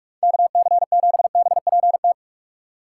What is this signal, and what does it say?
Morse code, heard as a keyed tone, about 47 wpm, 700 Hz. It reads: KC7BPT